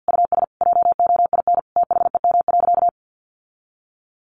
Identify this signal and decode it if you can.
Morse code, heard as a keyed tone, about 50 wpm, 700 Hz. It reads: US JOIN THEM.
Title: Sentences from Top 1400 Words 50wpm